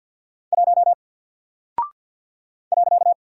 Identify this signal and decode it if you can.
Morse code, heard as a keyed tone, about 49 wpm, 700 Hz. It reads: 1 .
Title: Single Character (Letter:Number:Punctuation:Prosign) Alt 50wpm